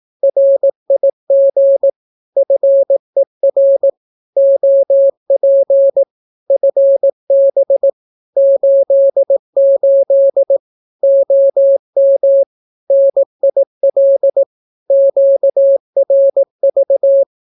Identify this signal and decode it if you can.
Morse code, heard as a keyed tone, about 18 wpm, 550 Hz. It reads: RIG FER OP FB 88 OM NIL QRV
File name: naukapodstaw / skroty / skroty